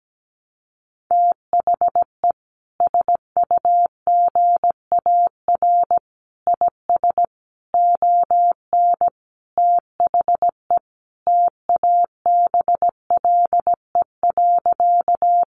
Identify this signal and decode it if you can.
Morse code, heard as a keyed tone, about 17 wpm, 700 Hz. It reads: THE SUGAR IS ON THE TABLE.
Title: Sentences from Top 1200 Words 17wpm